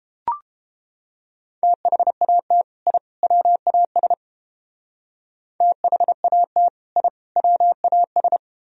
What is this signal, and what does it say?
Morse code, heard as a keyed tone, about 30 wpm, 700 Hz. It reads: THAT I WAS THAT I WAS